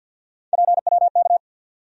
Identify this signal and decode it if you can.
Morse code, heard as a keyed tone, about 47 wpm, 700 Hz. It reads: PWK
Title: USA Call Sign Postfixes - 3 Letters 50wpm